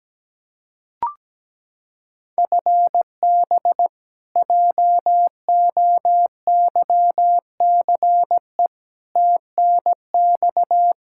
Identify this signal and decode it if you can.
Morse code, heard as a keyed tone, about 17 wpm, 700 Hz. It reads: FB JOYCE TNX
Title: K1USN SST QSO Practice 17wpm